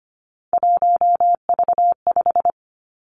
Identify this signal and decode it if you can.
Morse code, heard as a keyed tone, about 25 wpm, 700 Hz. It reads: 1V5
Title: Three Challenging Letter-Number Combinations 25wpm